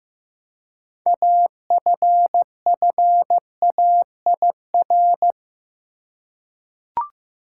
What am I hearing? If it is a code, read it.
Morse code, heard as a keyed tone, about 15 wpm, 700 Hz. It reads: AFFAIR
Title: Six Letter Words - 15wpm